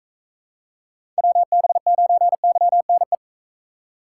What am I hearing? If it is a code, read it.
Morse code, heard as a keyed tone, about 40 wpm, 700 Hz. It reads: WB9YNE